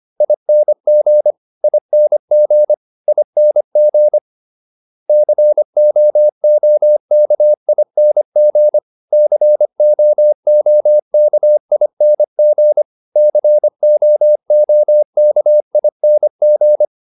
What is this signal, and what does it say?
Morse code, heard as a keyed tone, about 25 wpm, 600 Hz. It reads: ING ING ING COOKING COOKING COOKING